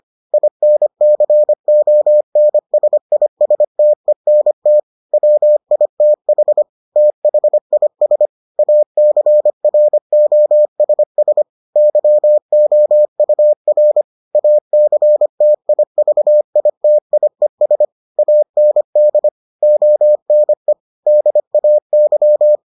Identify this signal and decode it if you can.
Morse code, heard as a keyed tone, about 25 wpm, 600 Hz. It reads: INCONSISTENT WITH THIS ACROSS YOUR ACTIVITIES AND ONE DAY